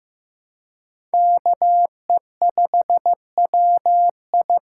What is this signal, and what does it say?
Morse code, heard as a keyed tone, about 15 wpm, 700 Hz. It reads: KE5WI
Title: USA Call Signs - 5 Characters - 4x Character Spacing - 15wpm